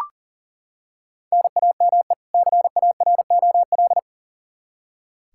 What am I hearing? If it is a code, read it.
Morse code, heard as a keyed tone, about 40 wpm, 700 Hz. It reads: NAME CAROL
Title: K1USN SST QSO Practice 40wpm